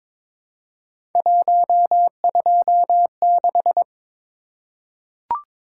Morse code, heard as a keyed tone, about 22 wpm, 700 Hz. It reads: 126
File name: practice / 22wpm / Three Letter-Number Combinations - 22wpm